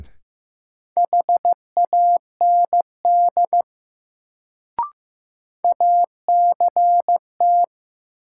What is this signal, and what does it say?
Morse code, heard as a keyed tone, about 15 wpm, 700 Hz. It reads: HAND ACT
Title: Sentences from Top 200 Words - New Word Drill - 15wpm